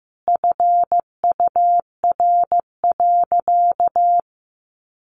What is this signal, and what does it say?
Morse code, heard as a keyed tone, about 15 wpm, 700 Hz. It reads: FUR.
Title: Easy Phrases 15wpm